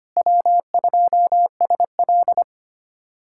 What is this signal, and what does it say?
Morse code, heard as a keyed tone, about 25 wpm, 700 Hz. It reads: W2SL
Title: USA Callsign DE Callsign 25wpm